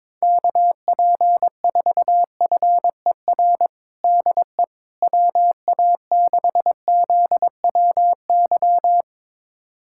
Morse code, heard as a keyed tone, about 21 wpm, 700 Hz. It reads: KP4FER DE WA6ZWY
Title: USA Callsign DE Callsign - 6 Characters - 22wpm